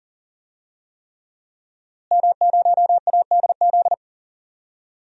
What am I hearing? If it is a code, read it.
Morse code, heard as a keyed tone, about 40 wpm, 700 Hz. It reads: M0ADZ
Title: DXCC-100 Call Signs - 5 Characters - Repeated 3x - 40wpm